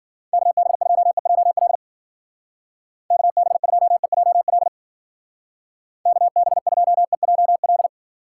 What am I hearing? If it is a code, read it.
Morse code, heard as a keyed tone, about 50 wpm, 700 Hz. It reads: KB2EJL KB2EJL KB2EJL